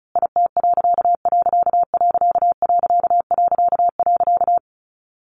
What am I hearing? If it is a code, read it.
Morse code, heard as a keyed tone, about 34 wpm, 700 Hz. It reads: IT......
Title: A Sound Of Thunder 35wpm